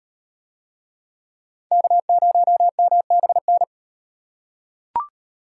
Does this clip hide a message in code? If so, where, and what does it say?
Yes: Morse code from 1.7 to 3.6 s, heard as a keyed tone, about 38 wpm, 700 Hz. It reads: K0MBN